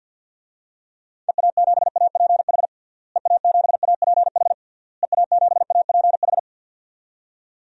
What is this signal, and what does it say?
Morse code, heard as a keyed tone, about 52 wpm, 700 Hz. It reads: EA7APH EA7APH EA7APH
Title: DXCC-100 Call Signs - 6 Characters - Repeated 3x - 50wpm